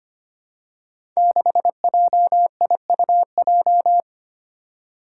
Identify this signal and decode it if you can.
Morse code, heard as a keyed tone, about 25 wpm, 700 Hz. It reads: 6JIUJ
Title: Five Challenging Letter-Number Combinations 25wpm